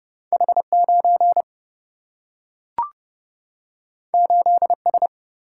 Morse code, heard as a keyed tone, about 30 wpm, 700 Hz. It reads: H 9 8 S
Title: Single-Letter Single-Number 30wpm